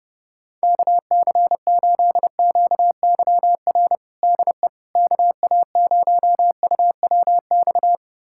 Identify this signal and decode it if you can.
Morse code, heard as a keyed tone, about 30 wpm, 700 Hz. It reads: KC8QYR DE KA0UWX
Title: USA Callsign DE Callsign - 6 Characters - 30wpm